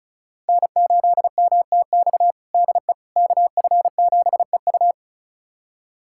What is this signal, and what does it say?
Morse code, heard as a keyed tone, about 36 wpm, 700 Hz. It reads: N8MTX DE KF7EU
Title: USA Callsign DE Callsign - 5 Characters - 35wpm